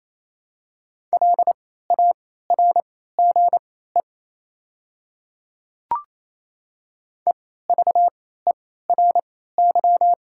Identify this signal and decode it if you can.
Morse code, heard as a keyed tone, about 27 wpm, spaced as for 16 wpm, 690 Hz. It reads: LARGE EVERY
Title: 3x Character Spacing - Top 500 Words - 28wpm